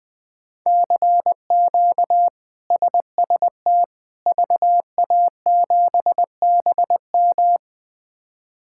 Morse code, heard as a keyed tone, about 20 wpm, 700 Hz. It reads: CQ SST VA7BM
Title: K1USN SST QSO Practice - 20wpm